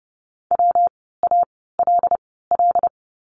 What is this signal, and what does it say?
Morse code, heard as a keyed tone, about 30 wpm, spaced as for 17 wpm, 700 Hz. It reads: WALL
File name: course / 30:10wpm / 115 Lesson 19f - Letter w - Words 30wpm